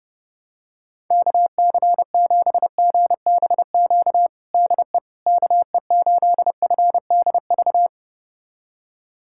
Morse code, heard as a keyed tone, about 30 wpm, 700 Hz. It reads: KC7GBQ DE KE8FDV
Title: USA Callsign DE Callsign - 6 Characters - 30wpm